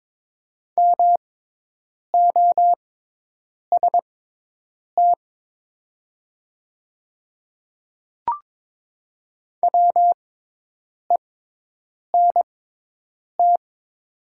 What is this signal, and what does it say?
Morse code, heard as a keyed tone, about 22 wpm, spaced as for 8 wpm, 700 Hz. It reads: MOST WENT
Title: 6x Character Spacing - Top 200 Words - 22wpm